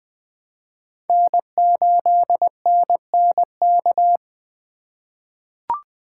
Morse code, heard as a keyed tone, about 20 wpm, 700 Hz. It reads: N8NNK
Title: USA Call Signs - 5 Characters - 3x Character Spacing - 20wpm